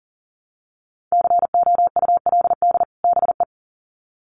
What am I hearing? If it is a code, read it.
Morse code, heard as a keyed tone, about 38 wpm, 700 Hz. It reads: COULD BE
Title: Sets of 2 Words 40wpm